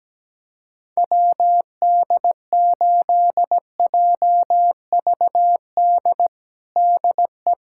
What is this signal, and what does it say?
Morse code, heard as a keyed tone, about 17 wpm, 700 Hz. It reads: WD8JVD DE
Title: USA Callsign DE Callsign 17wpm